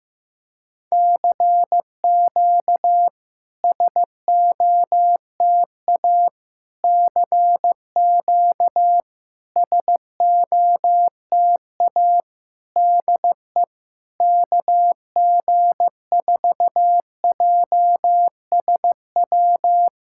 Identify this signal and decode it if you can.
Morse code, heard as a keyed tone, about 15 wpm, 700 Hz. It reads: CQ SOTA CQ SOTA DE KG4JSW
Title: SOTA QSO Practice 15wpm